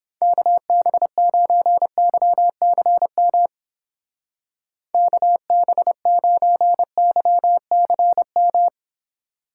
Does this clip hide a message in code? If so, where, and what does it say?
Yes: Morse code from 0.2 to 8.7 s, heard as a keyed tone, about 31 wpm, 700 Hz. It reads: KB9YCM KB9YCM